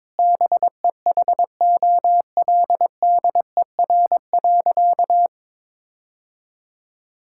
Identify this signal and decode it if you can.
Morse code, heard as a keyed tone, about 22 wpm, 700 Hz. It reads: BEHOLDER.
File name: practice / 22wpm / Sentences with 8-Word Phrases - 22wpm